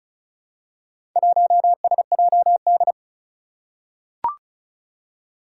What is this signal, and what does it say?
Morse code, heard as a keyed tone, about 36 wpm, 700 Hz. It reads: 1SJD
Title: Four Challenging Letter-Number Combinations 35wpm